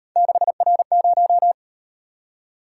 Morse code, heard as a keyed tone, about 37 wpm, 700 Hz. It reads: 6R0